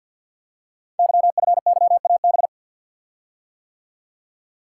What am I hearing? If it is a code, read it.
Morse code, heard as a keyed tone, about 45 wpm, 700 Hz. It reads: YFYAB
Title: Code Groups - 5 Letter:Numbers - 50wpm